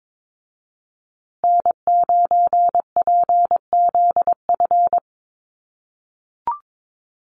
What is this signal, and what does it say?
Morse code, heard as a keyed tone, about 22 wpm, 700 Hz. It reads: N9PZF
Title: USA Call Signs Repeated 3x 22wpm